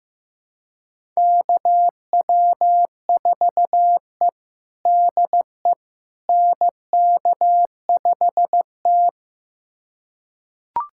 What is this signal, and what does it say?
Morse code, heard as a keyed tone, about 15 wpm, 700 Hz. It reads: KW4E DE NK5T
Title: USA Callsign DE Callsign - 4 Characters - 15wpm